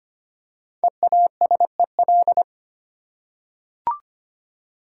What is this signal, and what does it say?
Morse code, heard as a keyed tone, about 25 wpm, 700 Hz. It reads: EASEL
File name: practice / 25wpm / Easy Words with Hints - 25wpm